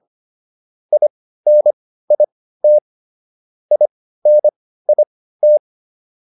Morse code, heard as a keyed tone, about 25 wpm, spaced as for 15 wpm, 600 Hz. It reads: INIT INIT